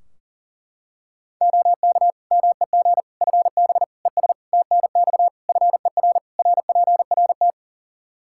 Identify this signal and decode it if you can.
Morse code, heard as a keyed tone, about 37 wpm, 700 Hz. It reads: OK MEG FB ES TNX FER RPRT
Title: Practice QSO - Exchange 2 - Info - 40wpm